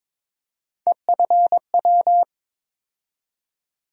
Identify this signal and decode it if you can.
Morse code, heard as a keyed tone, about 21 wpm, 700 Hz. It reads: EFW
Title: USA Call Sign Postfixes - 3 Letters 22wpm